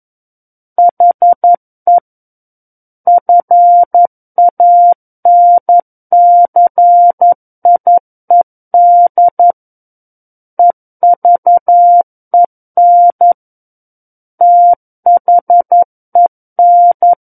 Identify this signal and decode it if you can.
Morse code, heard as a keyed tone, about 11 wpm, 700 Hz. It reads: HE FANCIED EVEN THEN
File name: 11 WPM